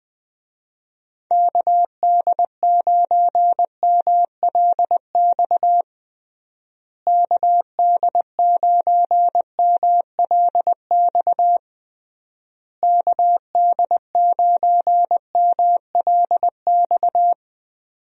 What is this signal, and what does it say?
Morse code, heard as a keyed tone, about 20 wpm, 700 Hz. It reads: KD9MLX KD9MLX KD9MLX
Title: USA Call Signs - 6 Characters - Repeated 3x - 20wpm